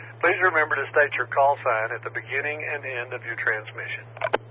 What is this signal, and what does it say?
Emergency Alert System (SAME) tones, heard as harsh buzzing bursts, with speech. It is heard from 0.3 to 3.7 s.